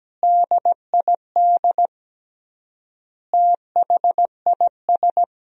Morse code, heard as a keyed tone, about 17 wpm, 700 Hz. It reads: DID THIS
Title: 3x Word-Spacing - Sentences from Top 400 Words - 17wpm